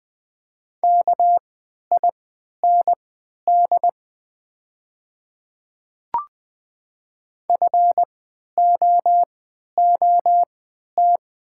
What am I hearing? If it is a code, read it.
Morse code, heard as a keyed tone, about 20 wpm, spaced as for 11 wpm, 700 Hz. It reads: KIND FOOT
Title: 3x Character Spacing - Top 500 Words - 20wpm